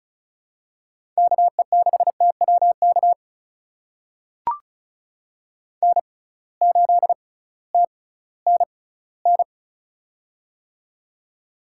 Morse code, heard as a keyed tone, about 36 wpm, spaced as for 19 wpm, 700 Hz. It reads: KE6TWK N8TNN
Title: USA Call Signs - 6x Character Spacing - 35wpm